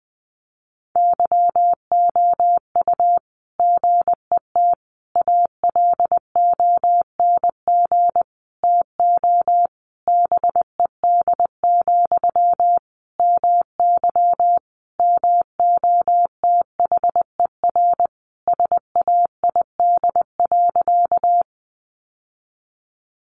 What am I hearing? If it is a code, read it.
Morse code, heard as a keyed tone, about 20 wpm, 700 Hz. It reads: YOU GET ALONG TO BED, MY MOTHER SAID.
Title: Sentences from Top 700 Words 20wpm